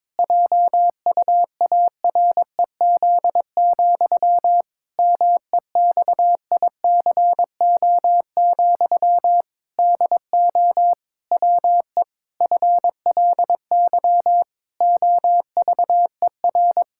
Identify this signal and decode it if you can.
Morse code, heard as a keyed tone, about 22 wpm, 700 Hz. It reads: JUAREZ, MEXICO, DO WE FLY OVER
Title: Pangram Challenge v2 - 22wpm